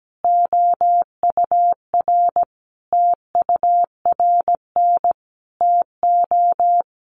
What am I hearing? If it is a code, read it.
Morse code, heard as a keyed tone, about 17 wpm, 700 Hz. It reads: OUR TURN TO
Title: Sentences from Top 1400 Words 17wpm